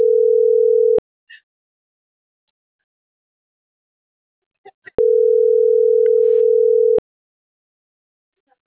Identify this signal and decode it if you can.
Ringback tone (North America), 440 + 480 Hz.